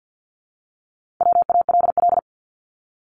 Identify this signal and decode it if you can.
Morse code, heard as a keyed tone, about 50 wpm, 700 Hz. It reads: WALL
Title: Multi-letter Phonemes 50wpm